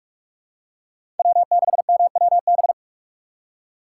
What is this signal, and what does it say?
Morse code, heard as a keyed tone, about 46 wpm, 700 Hz. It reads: W6MWB
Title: USA Call Signs - 5 Characters - 3x Character Spacing - 45wpm